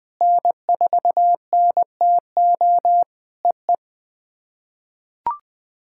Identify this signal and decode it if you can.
Morse code, heard as a keyed tone, about 20 wpm, 700 Hz. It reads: N4NTO EE
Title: POTA QSO Practice - 20wpm